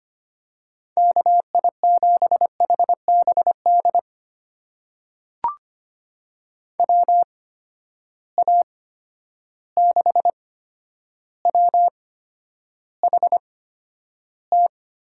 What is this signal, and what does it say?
Morse code, heard as a keyed tone, about 25 wpm, spaced as for 11 wpm, 700 Hz. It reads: KI7HBD WA6WHT
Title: USA Call Signs - 8x Character Spacing - 25wpm